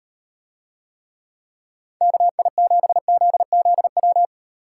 Morse code, heard as a keyed tone, about 38 wpm, 700 Hz. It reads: KI7ZZW